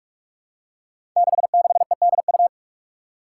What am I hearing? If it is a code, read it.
Morse code, heard as a keyed tone, about 46 wpm, 700 Hz. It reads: 66EDU
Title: Five Challenging Letter-Number Combinations 45wpm